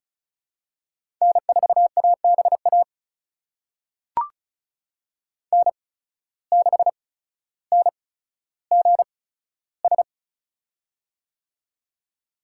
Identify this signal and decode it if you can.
Morse code, heard as a keyed tone, about 36 wpm, spaced as for 15 wpm, 700 Hz. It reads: N4ABA N6NGS
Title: USA Call Signs - 5 Characters - 8x Character Spacing - 35wpm